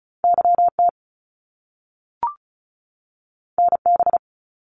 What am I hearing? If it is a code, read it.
Morse code, heard as a keyed tone, about 36 wpm, 700 Hz. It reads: YT NB
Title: Canadian Provinces and Territories 35wpm